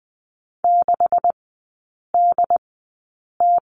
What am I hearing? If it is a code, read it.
Morse code, heard as a keyed tone, about 20 wpm, spaced as for 8 wpm, 700 Hz. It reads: 6DT
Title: 127 Lesson 21e - Number 6 - Three Characters 20wpm